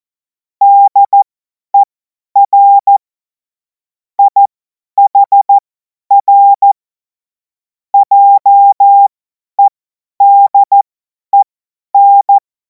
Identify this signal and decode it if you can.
Morse code, heard as a keyed tone, about 14 wpm, spaced as for 10 wpm, 800 Hz. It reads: DER IHR JEDEN
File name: Morsekurs / Witze / Witz 10. März 2026 50 F70